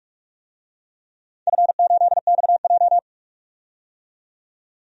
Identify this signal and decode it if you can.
Morse code, heard as a keyed tone, about 41 wpm, 700 Hz. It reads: F8XJ
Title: DXCC-100 Call Signs - 4 Characters - 45wpm